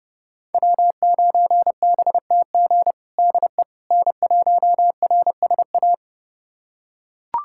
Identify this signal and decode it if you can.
Morse code, heard as a keyed tone, about 32 wpm, 700 Hz. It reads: W9BTG DE N1RSA
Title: USA Callsign DE Callsign - 5 Characters - 30wpm